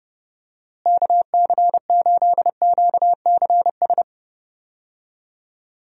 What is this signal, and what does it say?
Morse code, heard as a keyed tone, about 31 wpm, 700 Hz. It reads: KC8QCS
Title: USA Call Signs - 6 Characters - Repeated 3x - 30wpm